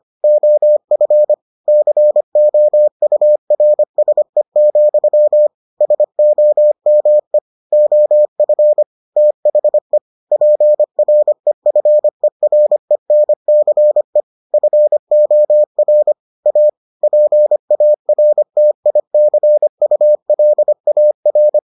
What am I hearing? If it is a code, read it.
Morse code, heard as a keyed tone, about 25 wpm, 600 Hz. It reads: OF COURSE, SOME OF THE PREFERENCE FOR A PARTICULAR